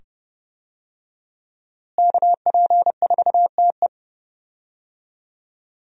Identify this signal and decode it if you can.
Morse code, heard as a keyed tone, about 30 wpm, 700 Hz. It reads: KP4TE
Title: DXCC-100 Call Signs - 5 Characters - Repeated 2x - 30wpm